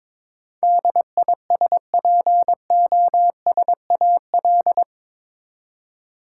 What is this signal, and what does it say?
Morse code, heard as a keyed tone, about 22 wpm, 700 Hz. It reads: DISPOSAL
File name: practice / 22wpm / Eight Letter Words - 22wpm